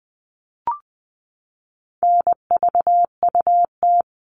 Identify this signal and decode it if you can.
Morse code, heard as a keyed tone, about 20 wpm, 700 Hz. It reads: NVUT
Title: All Cut Numbers - 4-Digit 20wpm